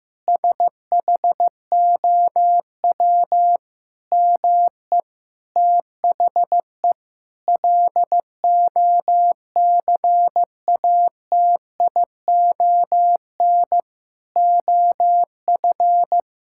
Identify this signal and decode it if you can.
Morse code, heard as a keyed tone, about 15 wpm, 700 Hz. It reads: SHOW ME THE LOCATION OF
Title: Sentences from Top 1000 Words - Review Part 6 - 15wpm